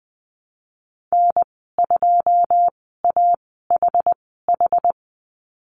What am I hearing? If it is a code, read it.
Morse code, heard as a keyed tone, about 19 wpm, spaced as for 14 wpm, 700 Hz. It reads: N2AHH
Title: 128 Lesson 21g - Number 6 - Call Signs 20wpm